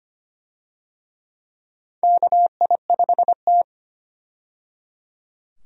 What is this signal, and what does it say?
Morse code, heard as a keyed tone, about 25 wpm, 700 Hz. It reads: KI5T
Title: USA Call Signs - 4 Characters - 25wpm